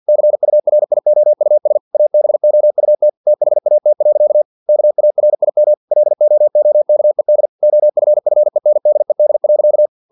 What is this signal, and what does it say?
Morse code, heard as a keyed tone, about 50 wpm, 600 Hz. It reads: CURIOUS ABOUT THAT. KARIM LOOKED OFFENDED.
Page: Never 49wpm